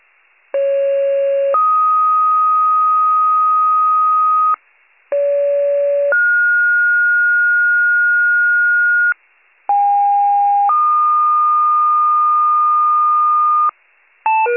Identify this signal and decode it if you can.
Two-tone page, heard as a one-second tone, then a three-second tone, about 550 then 1250 Hz, about 570 then 1430 Hz, about 810 then 1220 Hz. It is heard from 0.5 to 13.7 s.